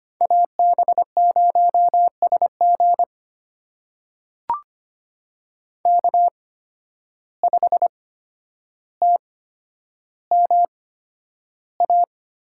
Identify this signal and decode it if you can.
Morse code, heard as a keyed tone, about 25 wpm, spaced as for 11 wpm, 700 Hz. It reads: AB0SG K5TMA
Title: USA Call Signs - 5 Characters - 8x Character Spacing - 25wpm